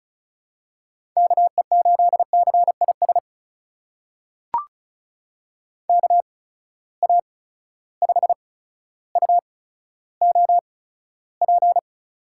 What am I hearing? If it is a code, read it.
Morse code, heard as a keyed tone, about 36 wpm, spaced as for 15 wpm, 700 Hz. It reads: KE8CIS KA5UOP
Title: USA Call Signs - 6 Characters - 8x Character Spacing - 35wpm